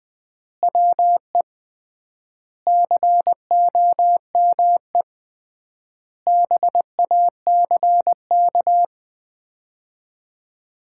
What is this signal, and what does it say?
Morse code, heard as a keyed tone, about 20 wpm, 700 Hz. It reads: WE COME BACK
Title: 3x Word-Spacing - Sets of 5 Words - 20wpm